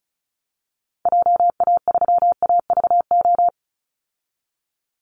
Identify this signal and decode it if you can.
Morse code, heard as a keyed tone, about 34 wpm, 700 Hz. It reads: JA3AVO
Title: Call Signs - 35wpm